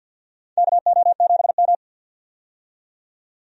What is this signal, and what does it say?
Morse code, heard as a keyed tone, about 47 wpm, 700 Hz. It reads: KO7M
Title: USA Call Signs - 4 Characters - 50wpm